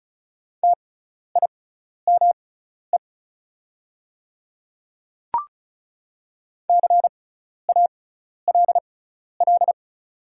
Morse code, heard as a keyed tone, about 36 wpm, spaced as for 12 wpm, 700 Hz. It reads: TIME CALL
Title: 6x Character Spacing - Top 200 Words - 35wpm